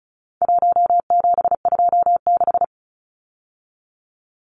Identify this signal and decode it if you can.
Morse code, heard as a keyed tone, about 34 wpm, 700 Hz. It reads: 1726
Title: Numbers - 4-Digit 35wpm